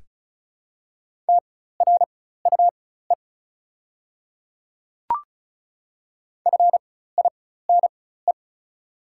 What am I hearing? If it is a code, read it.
Morse code, heard as a keyed tone, about 34 wpm, spaced as for 16 wpm, 690 Hz. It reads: TRUE FINE
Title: 4x Character Spacing - Top 500 Words - 35wpm